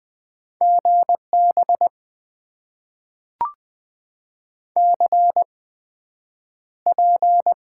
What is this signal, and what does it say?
Morse code, heard as a keyed tone, about 21 wpm, spaced as for 9 wpm, 700 Hz. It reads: GB CP
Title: Top 90 QSO Elements - 8x Character Spacing - 20wpm